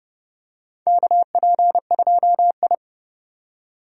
Morse code, heard as a keyed tone, about 29 wpm, 700 Hz. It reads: KP2I
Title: DXCC-100 Call Signs - 4 Characters - 30wpm